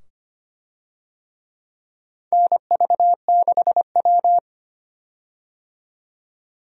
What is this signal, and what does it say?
Morse code, heard as a keyed tone, about 25 wpm, 700 Hz. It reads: NV6W
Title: USA Call Signs - 4 Characters - 25wpm